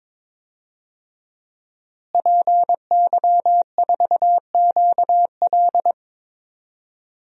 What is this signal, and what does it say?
Morse code, heard as a keyed tone, about 22 wpm, 700 Hz. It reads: PY4QL
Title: DXCC-100 Call Signs - 5 Characters - Repeated 2x - 22wpm